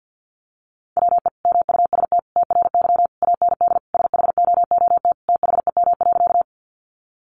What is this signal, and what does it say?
Morse code, heard as a keyed tone, about 50 wpm, 700 Hz. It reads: WE MUST TRY AND SHOOT THEM.
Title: Sentences from Top 1200 Words 50wpm